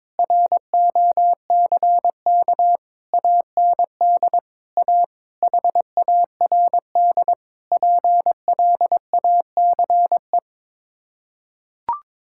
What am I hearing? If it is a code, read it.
Morse code, heard as a keyed tone, about 21 wpm, 700 Hz. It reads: ROCK AND A HARD PLACE
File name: practice / 22wpm / Sets of 8 Words - 22wpm